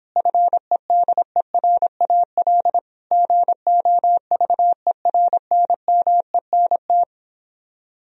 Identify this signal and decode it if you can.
Morse code, heard as a keyed tone, about 31 wpm, 700 Hz. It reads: FEDERAL GOVERNMENT